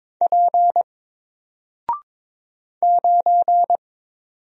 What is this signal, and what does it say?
Morse code, heard as a keyed tone, about 22 wpm, 700 Hz. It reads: P 9
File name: Single-Letter-Number-22wpm